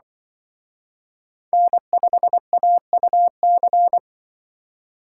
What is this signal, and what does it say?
Morse code, heard as a keyed tone, about 24 wpm, 700 Hz. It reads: N5AUC
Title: Speed Racing - USA Call Signs 20wpm